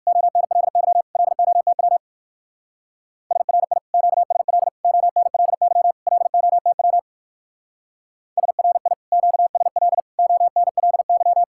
Morse code, heard as a keyed tone, about 60 wpm, 700 Hz. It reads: ONLY LOTW SRI QSL ONLY LOTW SRI QSL ONLY